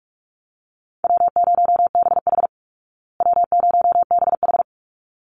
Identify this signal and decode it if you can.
Morse code, heard as a keyed tone, about 44 wpm, 700 Hz. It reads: W0BH W0BH
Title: Easy Call Signs Repeated 2x - 45wpm